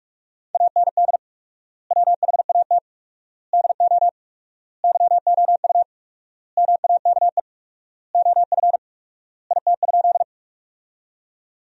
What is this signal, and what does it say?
Morse code, heard as a keyed tone, about 46 wpm, 700 Hz. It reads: AND WHAT DO YOU MAKE OF IT?
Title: 4x Word-Spacing - Sentences from Top 100 Words - 45wpm